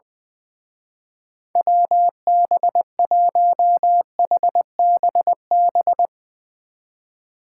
Morse code, heard as a keyed tone, about 22 wpm, 700 Hz. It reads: WB1HBB
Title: Speed Racing - USA 2x3 Call Signs 20wpm